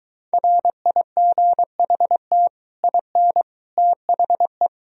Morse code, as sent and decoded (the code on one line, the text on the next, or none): .-. .. --. .... - / .. -. / - .... .
RIGHT IN THE